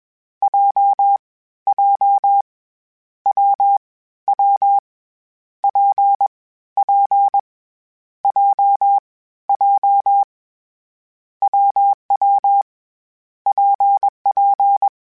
Morse code, heard as a keyed tone, about 21 wpm, 800 Hz. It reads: J J W W P P J J WW PP
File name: Difficult Combos / Difficult Combos-mp3s / WPJ WPJ1 @20